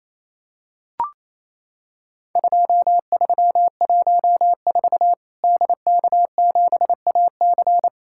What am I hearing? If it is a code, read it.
Morse code, heard as a keyed tone, about 28 wpm, 700 Hz. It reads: 2314 DK7AC
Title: Worked All Europe DX Contest - QTC Exchange - 28wpm